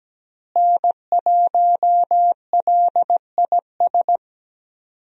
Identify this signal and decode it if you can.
Morse code, heard as a keyed tone, about 17 wpm, 700 Hz. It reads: N1LIS
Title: USA Call Signs - 5 Characters - Repeated 3x - 17wpm